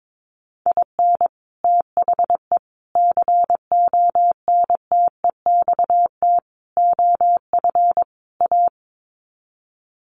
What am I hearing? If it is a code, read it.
Morse code, heard as a keyed tone, about 22 wpm, 700 Hz. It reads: IN THE CONTEXT OF A
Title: Sets of 5 Words Encore 22wpm